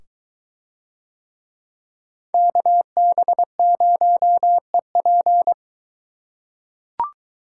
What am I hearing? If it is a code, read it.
Morse code, heard as a keyed tone, about 21 wpm, 700 Hz. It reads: KB0EP